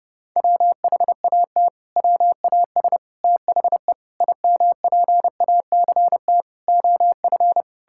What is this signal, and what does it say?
Morse code, heard as a keyed tone, about 30 wpm, 700 Hz. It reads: WHAT WAS THE IMPACT OF
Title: Sentences from Top 1000 Words - Review Part 6 - 30wpm